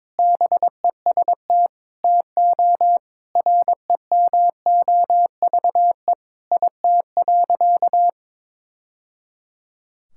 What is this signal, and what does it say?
Morse code, heard as a keyed tone, about 22 wpm, 700 Hz. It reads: BEST TO REMOVE IT.
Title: Sentences from Top 1000 Words - Review Part 5 - 22wpm